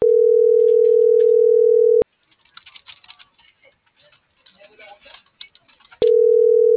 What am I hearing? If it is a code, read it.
Ringback tone (North America), 440 + 480 Hz.